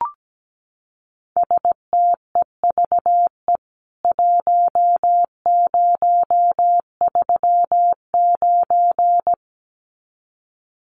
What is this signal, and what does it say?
Morse code, heard as a keyed tone, about 17 wpm, 700 Hz. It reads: STEVE 1039
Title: CWT Exchange 17wpm